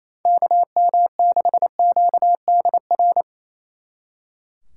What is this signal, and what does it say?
Morse code, heard as a keyed tone, about 37 wpm, 700 Hz. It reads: KM6QDR